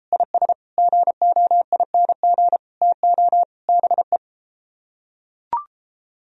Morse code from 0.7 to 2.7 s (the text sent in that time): GOING